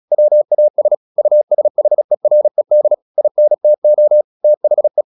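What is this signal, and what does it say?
Morse code, heard as a keyed tone, about 36 wpm, 600 Hz. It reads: WAS USHERED INTO THE
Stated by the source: Morse practice at 36 wpm, text of Ken Follett's novel Never, from CW med Gnister.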